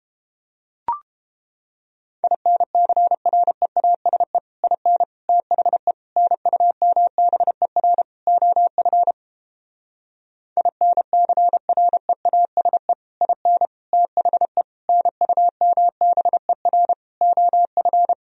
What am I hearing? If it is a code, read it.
Morse code, heard as a keyed tone, about 29 wpm, 700 Hz. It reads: INCREASE IN THE NUMBER OF INCREASE IN THE NUMBER OF